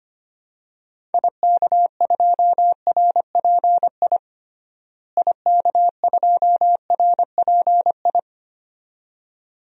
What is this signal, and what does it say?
Morse code, heard as a keyed tone, about 25 wpm, 700 Hz. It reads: IK2RPI IK2RPI
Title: DXCC-100 Call Signs - 6 Characters - Repeated 2x - 25wpm